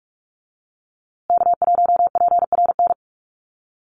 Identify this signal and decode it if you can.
Morse code, heard as a keyed tone, about 44 wpm, 700 Hz. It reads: K1PRN
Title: USA Call Signs Repeated 2x 45wpm